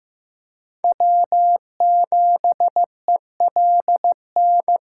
Morse code, heard as a keyed tone, about 15 wpm, 700 Hz. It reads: W7ELN
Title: USA Call Signs - 4x Character Spacing - 15wpm